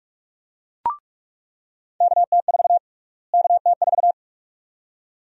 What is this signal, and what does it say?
Morse code, heard as a keyed tone, about 47 wpm, 700 Hz. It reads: KT4 KT4
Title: USA Call Sign Prefixes - Repeated 2x - 45wpm